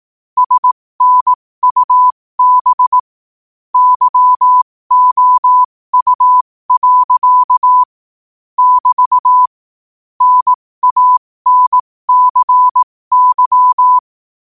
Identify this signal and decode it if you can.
Morse code, heard as a keyed tone, about 18 wpm, spaced as for 15 wpm, 1000 Hz. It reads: SNUB YOU. <BT> NANCY
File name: qotdfiles / 15wpm